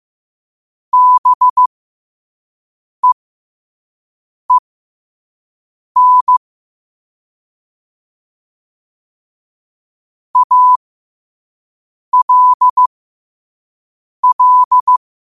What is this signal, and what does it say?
Morse code, heard as a keyed tone, about 15 wpm, spaced as for 5 wpm, 1000 Hz. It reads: BEEN ALL
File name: qotdfiles / 5wpm